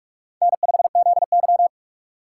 Morse code, heard as a keyed tone, about 44 wpm, 700 Hz. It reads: N5ZY
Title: USA Call Signs - 4 Characters - 2x Character Spacing - 45wpm